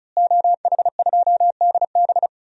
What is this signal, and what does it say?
Morse code, heard as a keyed tone, about 34 wpm, 700 Hz. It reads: OH2DB
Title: DXCC-100 Call Signs - 5 Characters - Repeated 2x - 35wpm